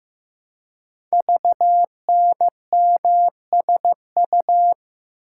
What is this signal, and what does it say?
Morse code, heard as a keyed tone, about 15 wpm, 700 Hz. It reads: VNMSU
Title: Code Groups - 5 Letters - 15wpm